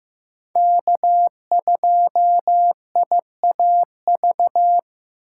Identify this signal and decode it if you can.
Morse code, heard as a keyed tone, about 15 wpm, 700 Hz. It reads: K2IAV
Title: USA Call Signs - 5 Characters - 6x Character Spacing - 15wpm